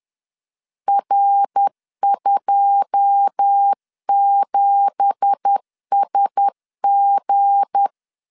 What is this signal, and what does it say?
Morse code, heard as a keyed tone, about 10 wpm, 800 Hz. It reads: R27SG
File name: CW / DET / 1 9 1500C S A 10wpm 4511